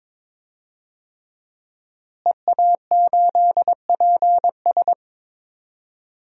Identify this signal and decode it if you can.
Morse code, heard as a keyed tone, about 22 wpm, 700 Hz. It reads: EA8PS
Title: DXCC-100 Call Signs - 5 Characters - Repeated 3x - 22wpm